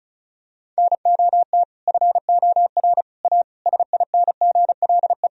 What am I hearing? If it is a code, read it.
Morse code, heard as a keyed tone, about 36 wpm, 700 Hz. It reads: NOT FOR A SINGLE
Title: Sentences from Top 1000 Words - Review Part 4 - 35wpm